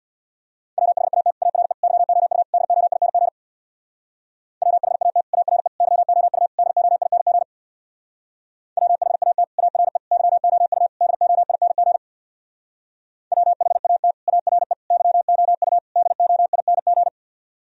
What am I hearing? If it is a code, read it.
Morse code, heard as a keyed tone, about 61 wpm, 700 Hz. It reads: WHAT ARE YOU DOING WHAT ARE YOU DOING WHAT ARE YOU DOING WHAT ARE YOU DOING
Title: Speed-Racing - Sets of 4 Words 50wpm